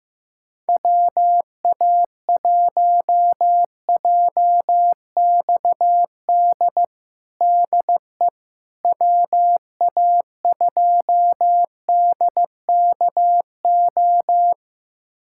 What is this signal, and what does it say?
Morse code, heard as a keyed tone, about 15 wpm, 700 Hz. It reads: WA1JXD DE WA2DKO
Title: USA Callsign DE Callsign - 6 Characters - 15wpm